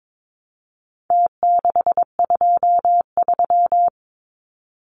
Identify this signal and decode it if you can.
Morse code, heard as a keyed tone, about 22 wpm, 700 Hz. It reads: T623
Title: Numbers - Contest Style - 4-Digit 22wpm